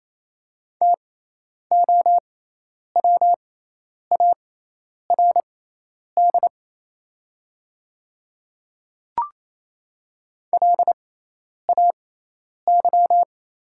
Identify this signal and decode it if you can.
Morse code, heard as a keyed tone, about 28 wpm, spaced as for 10 wpm, 700 Hz. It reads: TOWARD LAY
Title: 6x Character Spacing - Top 500 Words - 28wpm